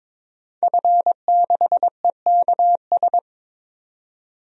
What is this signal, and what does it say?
Morse code, heard as a keyed tone, about 21 wpm, 700 Hz. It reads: F6EKS
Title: DXCC-100 Call Signs - 5 Characters - 22wpm